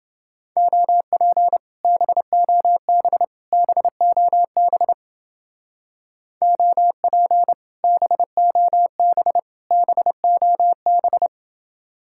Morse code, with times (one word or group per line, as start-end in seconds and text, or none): 0.6-1.6 OP
1.8-3.2 BOB
3.5-4.9 BOB
6.4-7.5 OP
7.8-9.4 BOB
9.7-11.3 BOB